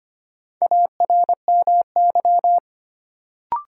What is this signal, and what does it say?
Morse code, heard as a keyed tone, about 25 wpm, 700 Hz. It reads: ARMY
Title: Sentences from Top 800 Words - New Word Drill - 25wpm